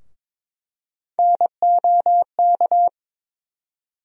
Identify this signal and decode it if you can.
Morse code, heard as a keyed tone, about 22 wpm, 700 Hz. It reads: NOK
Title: USA Call Sign Postfixes - 3 Letters 22wpm